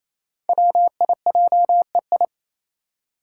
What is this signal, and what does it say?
Morse code, heard as a keyed tone, about 27 wpm, 700 Hz. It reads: WIJEI
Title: Five Challenging Letter-Number Combinations 28wpm